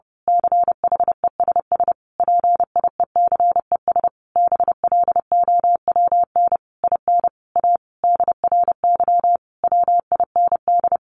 Morse code, heard as a keyed tone, about 30 wpm, 700 Hz. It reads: CHESS PIECES BLOWN IN A DRY WIND
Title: A Sound Of Thunder 30wpm